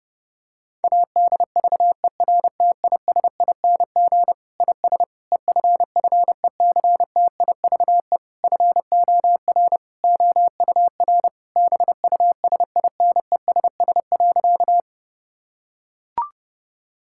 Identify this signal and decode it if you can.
Morse code, heard as a keyed tone, about 28 wpm, 700 Hz. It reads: ADVERTISING IS EFFECTIVE FOR OUR BUSINESS.
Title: Sentences from Top 1000 Words - Review Part 6 - 30wpm